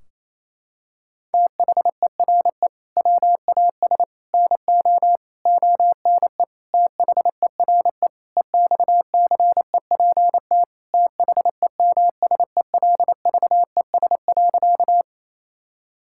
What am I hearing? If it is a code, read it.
Morse code, heard as a keyed tone, about 27 wpm, 700 Hz. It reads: THERE WAS NO ONE THERE EXCEPT THEMSELVES.
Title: Sentences from Top 1600 Words 28wpm